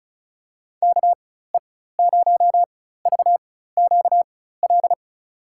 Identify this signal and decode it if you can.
Morse code, heard as a keyed tone, about 34 wpm, spaced as for 16 wpm, 700 Hz. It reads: KE0VQL
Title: USA Call Signs - 6 Characters - 4x Character Spacing - 35wpm